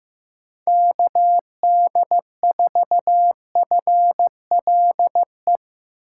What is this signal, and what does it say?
Morse code, heard as a keyed tone, about 16 wpm, 700 Hz. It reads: KD4FLE
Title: USA Call Signs - 6 Characters - 2x Character Spacing - 15wpm